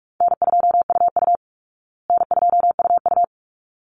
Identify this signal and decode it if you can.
Morse code, heard as a keyed tone, about 44 wpm, 700 Hz. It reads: N2UU N2UU
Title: Easy Call Signs Repeated 2x - 45wpm